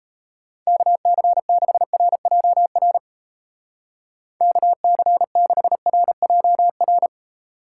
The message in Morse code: KC6RJR KC6RJR

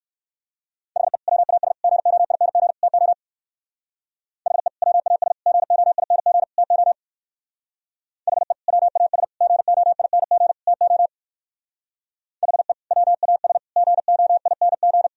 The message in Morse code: HE WAS GOING TO HE WAS GOING TO HE WAS GOING TO HE WAS GOING

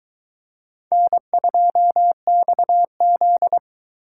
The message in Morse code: N2XZ

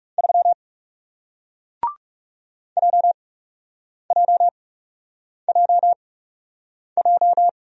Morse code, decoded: 3 JJJJ

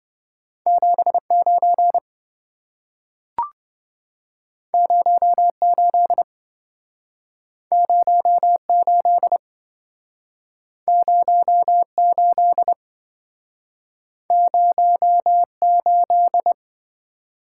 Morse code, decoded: 79 08 08 08 08

